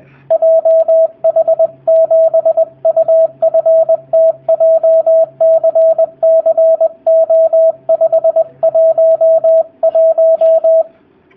Morse code, heard as a keyed tone, about 20 wpm, 650 Hz, with radio noise. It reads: JH7UFT JCCO511
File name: jh8cbh / 2019cw / sp20- w